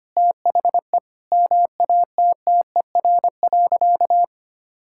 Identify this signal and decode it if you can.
Morse code, heard as a keyed tone, about 25 wpm, 700 Hz. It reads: THE MATTER.
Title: Sentences from Top 1000 Words - Review Part 6 - 25wpm